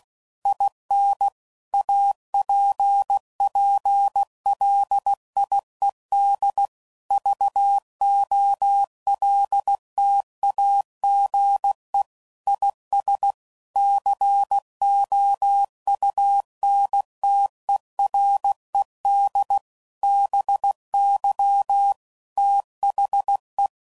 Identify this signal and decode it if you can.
Morse code, heard as a keyed tone, about 16 wpm, 800 Hz. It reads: IN APPLIED VOLTAGE IS COUNTERED BY THE
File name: CW Code Groups @15WPM / Technical Subjects / XC